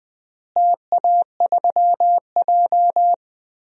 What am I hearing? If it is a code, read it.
Morse code, heard as a keyed tone, about 20 wpm, 700 Hz. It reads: TA3J
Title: DXCC-100 Call Signs - 4 Characters - Repeated 3x - 20wpm